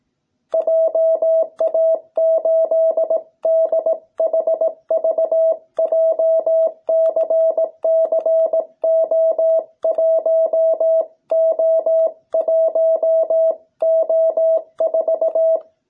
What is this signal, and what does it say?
Morse code, heard as a keyed tone, about 18 wpm, 660 Hz. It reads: JA8DHV JCCO1O1O4